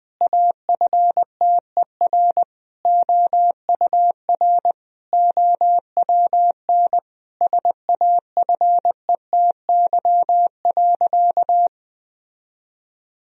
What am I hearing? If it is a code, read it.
Morse code, heard as a keyed tone, about 21 wpm, 700 Hz. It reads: AFTER OUR OWN SAFETY.
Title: Sentences from Top 1000 Words - Review Part 4 - 20wpm